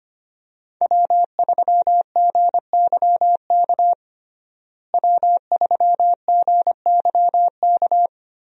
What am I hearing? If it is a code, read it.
Morse code, heard as a keyed tone, about 25 wpm, 700 Hz. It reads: W3GYK W3GYK
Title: USA Call Signs - 5 Characters - Repeated 3x - 25wpm